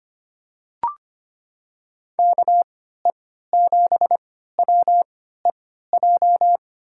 Morse code, heard as a keyed tone, about 25 wpm, spaced as for 14 wpm, 700 Hz. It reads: KE7WEJ
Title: USA Call Signs - 6 Characters - 3x Character Spacing - 25wpm